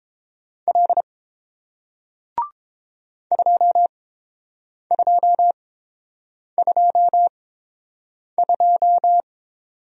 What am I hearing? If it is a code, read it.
Morse code, heard as a keyed tone, about 30 wpm, spaced as for 7 wpm, 700 Hz. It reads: L 2222